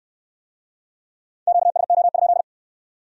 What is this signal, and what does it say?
Morse code, heard as a keyed tone, about 64 wpm, 700 Hz. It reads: QSO?